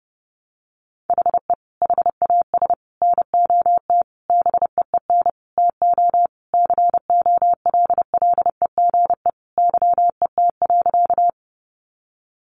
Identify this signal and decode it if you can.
Morse code, heard as a keyed tone, about 28 wpm, 700 Hz. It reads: HE HAS NOT BEEN TO COLLEGE YET.
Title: Sentences from Top 1300 Words 30wpm